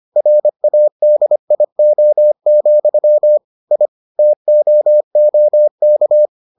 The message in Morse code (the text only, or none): RADIO, I TOOK